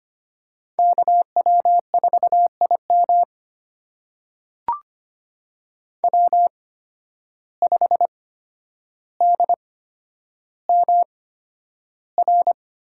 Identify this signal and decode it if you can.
Morse code, heard as a keyed tone, about 25 wpm, spaced as for 11 wpm, 700 Hz. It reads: KW4IM W5DMR
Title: USA Call Signs - 5 Characters - 8x Character Spacing - 25wpm